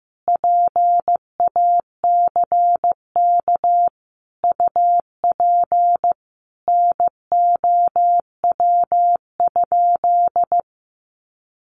Morse code, heard as a keyed tone, about 15 wpm, 700 Hz. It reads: PACK UP NOW?
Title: Sentences from Top 1300 Words 15wpm